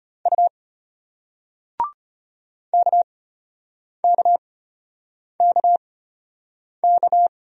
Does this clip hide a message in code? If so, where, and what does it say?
Yes: Morse code from 0.3 to 7.3 s, heard as a keyed tone, about 34 wpm, spaced as for 8 wpm, 700 Hz. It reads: U KKKK